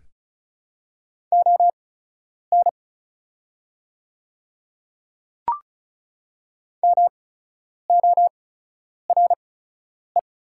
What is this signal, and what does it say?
Morse code, heard as a keyed tone, about 34 wpm, spaced as for 10 wpm, 690 Hz. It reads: ON MORE
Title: 8x Character Spacing - Top 100 Words - 35wpm